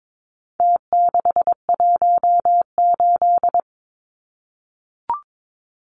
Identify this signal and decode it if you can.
Morse code, heard as a keyed tone, about 22 wpm, 700 Hz. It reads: T618
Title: Numbers - Contest Style - 4-Digit 22wpm